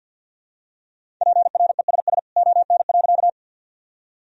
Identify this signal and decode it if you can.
Morse code, heard as a keyed tone, about 53 wpm, 700 Hz. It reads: PRESS ON.
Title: Sentences from Top 1000 Words - Review Part 1 - 50wpm